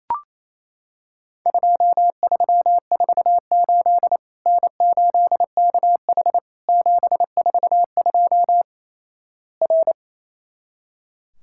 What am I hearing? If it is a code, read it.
Morse code, heard as a keyed tone, about 28 wpm, 700 Hz. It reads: 2348 N8KH 742 R
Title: Worked All Europe DX Contest - QTC Exchange - 28wpm